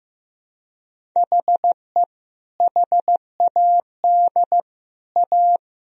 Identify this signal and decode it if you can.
Morse code, heard as a keyed tone, about 15 wpm, 700 Hz. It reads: HE HAD A
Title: Sentences from Top 1000 Words - Review Part 1 - 15wpm